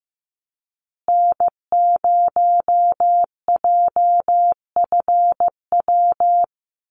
Morse code, heard as a keyed tone, about 15 wpm, 700 Hz. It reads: N0JFW
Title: USA Call Signs Repeated 2x 15wpm